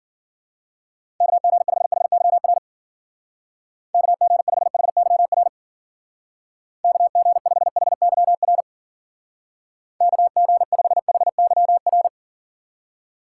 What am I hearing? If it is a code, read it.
Morse code, heard as a keyed tone, about 52 wpm, 700 Hz. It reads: KG5HYR KG5HYR KG5HYR KG5HYR